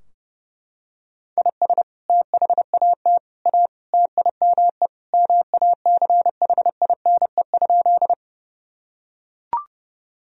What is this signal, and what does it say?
Morse code, heard as a keyed tone, about 30 wpm, 700 Hz. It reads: IS THAT A TIME MACHINE?
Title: Sentences from Top 1000 Words - Review Part 2 - 30wpm